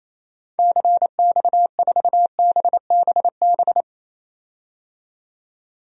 Morse code, heard as a keyed tone, about 28 wpm, 700 Hz. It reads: CX4BBB
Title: DXCC-100 Call Signs - 6 Characters - 28wpm